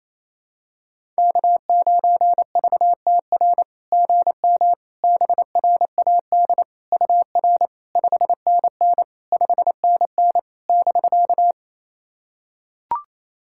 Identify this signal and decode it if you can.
Morse code, heard as a keyed tone, about 28 wpm, 700 Hz. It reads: K9VTR GM BRAD UR 5NN 5NN <BK>
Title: SOTA QSO Practice 28wpm